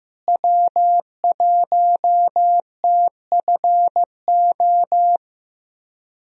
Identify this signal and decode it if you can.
Morse code, heard as a keyed tone, about 15 wpm, 700 Hz. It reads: W1TFO
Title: USA Call Signs - 5 Characters - Repeated 2x - 15wpm